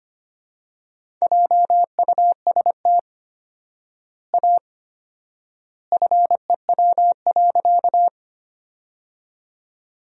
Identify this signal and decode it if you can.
Morse code, heard as a keyed tone, about 25 wpm, 700 Hz. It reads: JUST A FEW.
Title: 4x Word-Spacing - Sentences from Top 300 Words - 25wpm